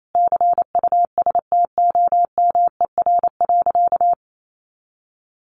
Morse code, heard as a keyed tone, about 28 wpm, 700 Hz. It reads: CUSTOMER.
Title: Sentences from Top 1100 Words 28wpm